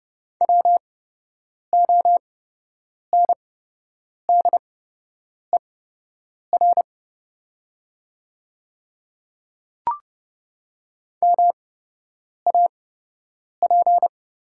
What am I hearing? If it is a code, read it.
Morse code, heard as a keyed tone, about 31 wpm, spaced as for 8 wpm, 700 Hz. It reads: WONDER MAP
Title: 8x Character Spacing - Top 500 Words - 30wpm